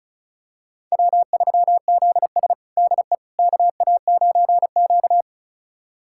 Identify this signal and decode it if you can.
Morse code, heard as a keyed tone, about 34 wpm, 700 Hz. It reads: W3ZS DE KA9Q
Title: USA Callsign DE Callsign - 4 Characters - 35wpm